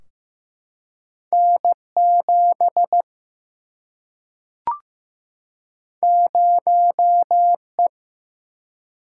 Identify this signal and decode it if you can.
Morse code, heard as a keyed tone, about 15 wpm, 700 Hz. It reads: N 7 0 E
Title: Single-Letter Single-Number 15wpm